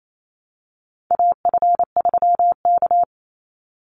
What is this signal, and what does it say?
Morse code, heard as a keyed tone, about 28 wpm, 700 Hz. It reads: AF3K
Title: Call Signs Repeated 2x - 28wpm